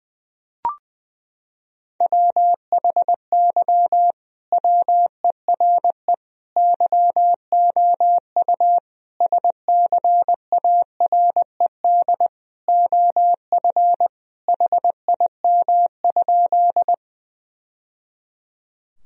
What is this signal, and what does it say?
Morse code, heard as a keyed tone, about 20 wpm, 700 Hz. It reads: WHY WERE YOU SCARED OF HIM?
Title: Sentences from Top 1000 Words - Review Part 4 - 20wpm